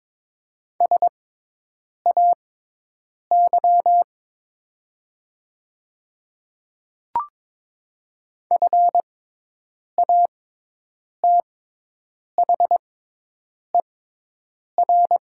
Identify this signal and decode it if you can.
Morse code, heard as a keyed tone, about 22 wpm, spaced as for 8 wpm, 700 Hz. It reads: SAY FATHER
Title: 6x Character Spacing - Top 200 Words - 22wpm